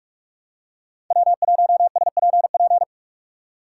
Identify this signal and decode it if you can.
Morse code, heard as a keyed tone, about 47 wpm, 700 Hz. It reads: W1SPP
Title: USA Call Signs - 5 Characters - 3x Character Spacing - 45wpm